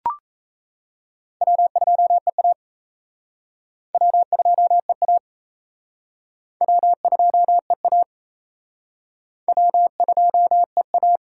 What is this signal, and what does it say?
Morse code, heard as a keyed tone, about 36 wpm, 700 Hz. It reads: W2EA W2EA W2EA W2EA